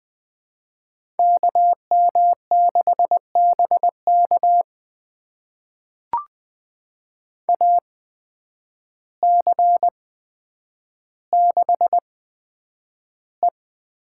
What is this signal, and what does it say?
Morse code, heard as a keyed tone, about 21 wpm, spaced as for 9 wpm, 700 Hz. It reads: KM6BK AC6E
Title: USA Call Signs - 5 Characters - 8x Character Spacing - 20wpm